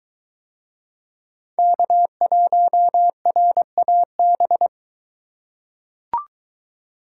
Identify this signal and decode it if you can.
Morse code, heard as a keyed tone, about 22 wpm, 700 Hz. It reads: K1RAB